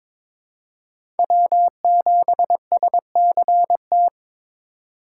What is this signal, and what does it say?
Morse code, heard as a keyed tone, about 22 wpm, 700 Hz. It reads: W7SCT
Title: USA Call Signs - 5 Characters - 2x Character Spacing - 22wpm